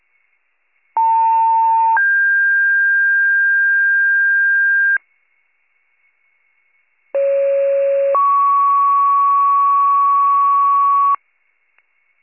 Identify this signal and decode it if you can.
Two-tone page, heard as a one-second tone, then a three-second tone, about 900 then 1600 Hz, about 550 then 1120 Hz. It is heard from 1.0 to 11.2 s.